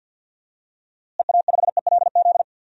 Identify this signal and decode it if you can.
Morse code, heard as a keyed tone, about 51 wpm, 700 Hz. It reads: EA5ELZ
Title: DXCC-100 Call Signs - 6 Characters - Repeated 3x - 50wpm